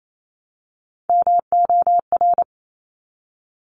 Morse code, heard as a keyed tone, about 28 wpm, 700 Hz. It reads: MOR
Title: Top Three Letter Combinations in Words 28wpm